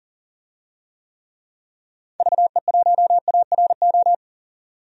Morse code, heard as a keyed tone, about 38 wpm, 700 Hz. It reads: VE1ARO
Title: DXCC-100 Call Signs - 6 Characters - 40wpm